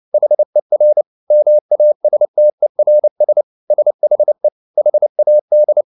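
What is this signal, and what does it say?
Morse code, heard as a keyed tone, about 29 wpm, 600 Hz. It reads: HER MASTERS SHE HAD
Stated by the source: Morse code at 29 wpm, sending Ken Follett's novel Never in English, from CW med Gnister.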